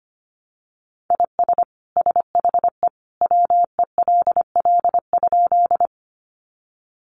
Morse code, heard as a keyed tone, about 25 wpm, 700 Hz. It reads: IS SHE WELL?
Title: Sentences from Top 200 Words 25wpm